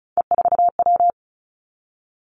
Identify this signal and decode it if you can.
Morse code, heard as a keyed tone, about 36 wpm, 700 Hz. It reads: E4W
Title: Three Challenging Letter-Number Combinations 35wpm